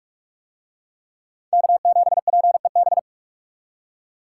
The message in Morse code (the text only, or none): K7PEB